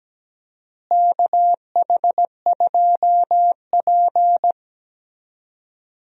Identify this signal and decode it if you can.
Morse code, heard as a keyed tone, about 17 wpm, 700 Hz. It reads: KH2P
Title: USA Call Signs - 4 Characters - Repeated 2x - 17wpm